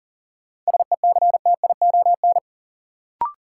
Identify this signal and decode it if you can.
Morse code, heard as a keyed tone, about 38 wpm, 700 Hz. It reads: SECTION
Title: Sentences from Top 1000 Words - Review Part 5 - New Word Drill - 40wpm